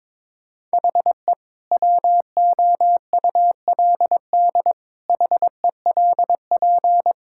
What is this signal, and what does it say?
Morse code, heard as a keyed tone, about 22 wpm, 700 Hz. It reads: HE WOULD HELP
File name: practice / 22wpm / Sentences with 2-Word Phrases - 22wpm